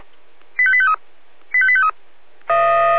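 Five-tone selective call (ZVEI), heard as quick runs of five tones, tones of about 70 ms. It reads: 85853 85853
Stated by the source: ZVEI five-tone call 85882, received by SDR.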